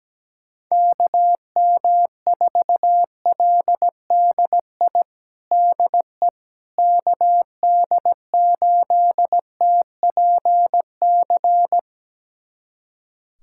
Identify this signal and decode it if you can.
Morse code, heard as a keyed tone, about 17 wpm, 700 Hz. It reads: KM4LDI DE KD8TPC
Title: USA Callsign DE Callsign - 6 Characters - 17wpm